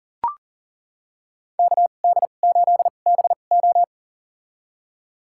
Morse code, heard as a keyed tone, about 37 wpm, spaced as for 28 wpm, 700 Hz. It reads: KD8BO
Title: USA Call Signs - 2x Character Spacing - 40wpm